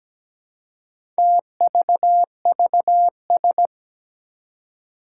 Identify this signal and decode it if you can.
Morse code, heard as a keyed tone, about 17 wpm, 700 Hz. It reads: TVVS
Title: Four Challenging Letter-Number Combinations 17wpm